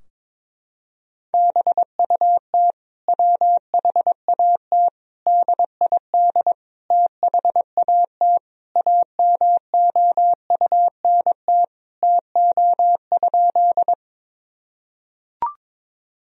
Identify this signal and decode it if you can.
Morse code, heard as a keyed tone, about 22 wpm, 700 Hz. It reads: BUT WHAT DID THAT AMOUNT TO?
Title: Sentences from Top 1000 Words - Review Part 4 - 22wpm